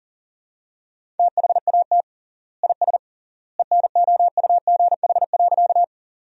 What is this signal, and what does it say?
Morse code, heard as a keyed tone, about 40 wpm, 700 Hz. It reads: THAT IS ENOUGH.
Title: 3x Word-Spacing - Sentences from Top 400 Words - 40wpm